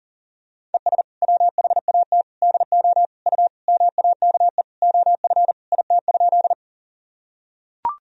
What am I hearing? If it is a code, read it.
Morse code, heard as a keyed tone, about 40 wpm, 700 Hz. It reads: ES WHAT DO U MAKE OF IT?
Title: Sentences from Top 100 Words and Abbreviations 40wpm